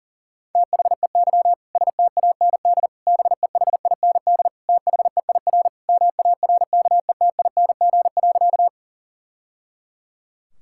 Morse code, heard as a keyed tone, about 37 wpm, 700 Hz. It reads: THEY STAND BEHIND THEIR MARKETING.
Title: Sentences from Top 1000 Words - Review Part 5 - 40wpm